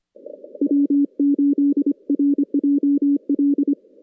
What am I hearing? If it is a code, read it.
Morse code, heard as a keyed tone, about 23 wpm, 340 Hz. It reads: W8RJL
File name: contests / CW-SS-k5tr- -2100